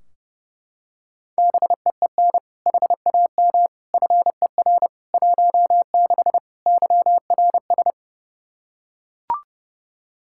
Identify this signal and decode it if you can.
Morse code, heard as a keyed tone, about 30 wpm, 700 Hz. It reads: BEEN HAM FER 16 YRS
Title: Practice QSO - Exchange 3 - More Info - 30wpm